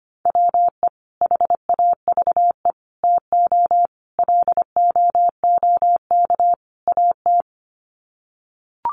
Morse code, heard as a keyed tone, about 25 wpm, 700 Hz. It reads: WE HAVE TO LOOK AT
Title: Sets of 5 Words Encore2 25wpm